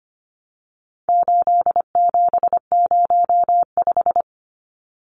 Morse code, heard as a keyed tone, about 25 wpm, 700 Hz. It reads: 8705
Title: Numbers - 4-Digit 25wpm